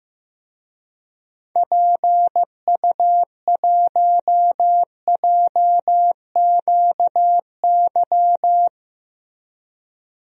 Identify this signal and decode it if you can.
Morse code, heard as a keyed tone, about 15 wpm, 700 Hz. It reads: PU1JQY
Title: DXCC-100 Call Signs - 6 Characters - Repeated 3x - 15wpm